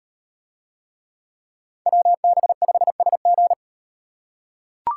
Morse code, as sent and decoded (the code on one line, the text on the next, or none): .-- -... ..... ... --.
WB5SG